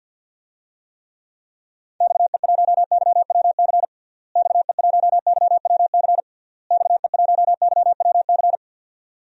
Morse code, heard as a keyed tone, about 52 wpm, 700 Hz. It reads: XE1YWC XE1YWC XE1YWC
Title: DXCC-100 Call Signs - 6 Characters - Repeated 3x - 50wpm